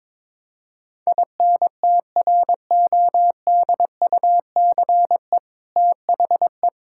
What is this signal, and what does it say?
Morse code, heard as a keyed tone, about 22 wpm, 700 Hz. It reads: INTRODUCE THE
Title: Sentences from Top 1600 Words 22wpm